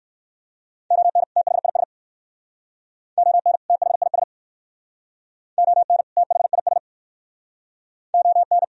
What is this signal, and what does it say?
Morse code, heard as a keyed tone, about 63 wpm, 700 Hz. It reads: ON THIS ON THIS ON THIS ON